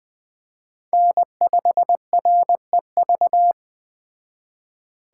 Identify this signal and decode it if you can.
Morse code, heard as a keyed tone, about 20 wpm, 700 Hz. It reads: N5REV
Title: Speed Racing - USA 1x3 Call Signs 17wpm